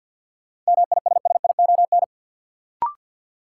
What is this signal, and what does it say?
Morse code, heard as a keyed tone, about 53 wpm, 700 Hz. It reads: MISSION
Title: Sentences from Top 1600 Words - New Word Drill Alt - 50wpm